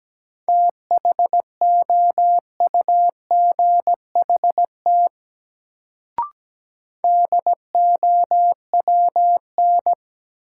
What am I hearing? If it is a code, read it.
Morse code, heard as a keyed tone, about 17 wpm, 700 Hz. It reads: THOUGHT DOWN
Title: Sentences from Top 300 Words - New Word Drill - 17wpm